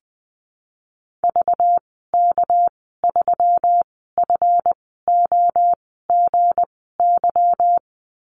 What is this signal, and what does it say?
Morse code, heard as a keyed tone, about 20 wpm, spaced as for 14 wpm, 700 Hz. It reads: VK3FOGY
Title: 213 Lesson 34g - Number 8 - Call Signs 20wpm